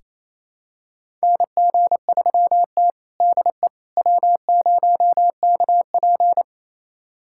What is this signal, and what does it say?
Morse code, heard as a keyed tone, about 28 wpm, 700 Hz. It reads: NG3T DE W0KP
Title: USA Callsign DE Callsign - 4 Characters - 28wpm